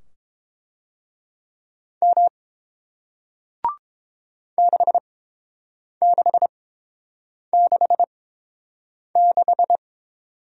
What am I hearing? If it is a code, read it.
Morse code, heard as a keyed tone, about 30 wpm, spaced as for 7 wpm, 700 Hz. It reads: M 6666